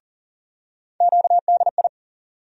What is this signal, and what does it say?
Morse code, heard as a keyed tone, about 38 wpm, 700 Hz. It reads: QDI
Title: USA Call Sign Postfixes - 3 Letters 40wpm